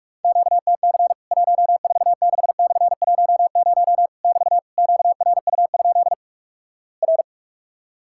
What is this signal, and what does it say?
Morse code, heard as a keyed tone, about 46 wpm, 700 Hz. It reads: QTC 146/10 <BT> QRU? R
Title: Worked All Europe DX Contest - QTC Exchange - 45wpm